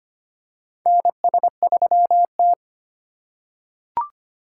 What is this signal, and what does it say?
Morse code, heard as a keyed tone, about 25 wpm, 700 Hz. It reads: NS3T
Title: USA Call Signs - 4 Characters - 3x Character Spacing - 25wpm